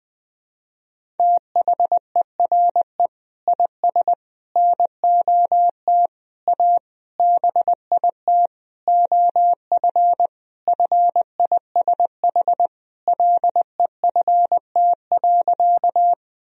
Morse code, heard as a keyed tone, about 20 wpm, 700 Hz. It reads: THERE IS NOT A BIT OF FISH LEFT.
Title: Sentences from Top 1000 Words - Review Part 1 - 20wpm